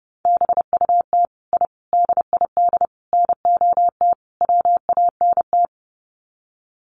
Morse code, heard as a keyed tone, about 30 wpm, 700 Hz. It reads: BUT I DID NOT WANT
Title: Sets of 5 Words Encore 30wpm